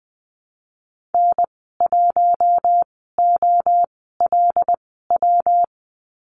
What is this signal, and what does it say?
Morse code, heard as a keyed tone, about 20 wpm, spaced as for 15 wpm, 700 Hz. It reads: N1OLW
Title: 122 Lesson 20g - Number 3 - Call Signs 20wpm